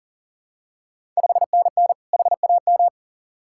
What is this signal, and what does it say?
Morse code, heard as a keyed tone, about 37 wpm, 700 Hz. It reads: 5NN HAM
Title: Ontario QSO Party Exchange 40wpm